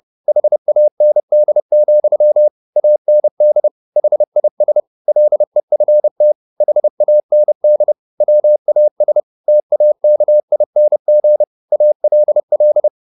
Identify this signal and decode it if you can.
Morse code, heard as a keyed tone, about 30 wpm, 600 Hz. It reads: HAND, AND HIS LEFT HAND WAS TAKING ALL